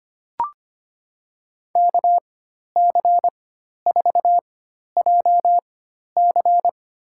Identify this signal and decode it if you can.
Morse code, heard as a keyed tone, about 25 wpm, spaced as for 12 wpm, 700 Hz. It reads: KC4JC
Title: USA Call Signs - 5 Characters - 4x Character Spacing - 25wpm